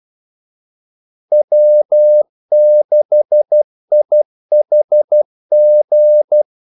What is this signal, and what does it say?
Morse code, heard as a keyed tone, about 12 wpm, 600 Hz. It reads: W6IHG
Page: 2 ekstra ordmellemrum mellem kaldesignalerne